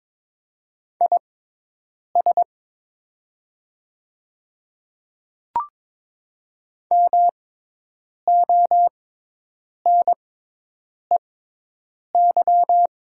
Morse code, heard as a keyed tone, about 21 wpm, spaced as for 8 wpm, 700 Hz. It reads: IS MONEY
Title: 6x Character Spacing - Top 500 Words - 22wpm